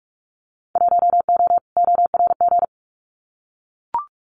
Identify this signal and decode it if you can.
Morse code, heard as a keyed tone, about 44 wpm, 700 Hz. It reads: 1O ORG
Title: Winter Field Day Exchanges 45wpm